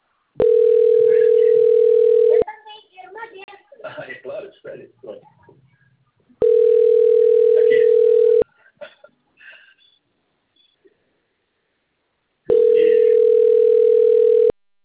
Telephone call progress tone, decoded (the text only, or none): ringback tone